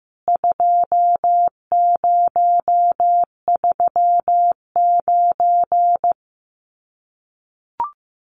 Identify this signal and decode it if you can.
Morse code, heard as a keyed tone, about 15 wpm, 700 Hz. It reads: 2039
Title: CWT Exchange 15wpm